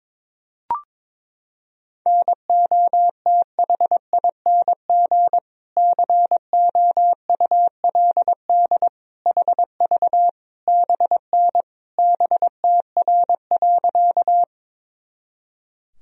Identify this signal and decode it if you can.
Morse code, heard as a keyed tone, about 21 wpm, 700 Hz. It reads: NOTHING COULD HV BN BTR.
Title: Sentences from Top 500 Words and Abbreviations 22wpm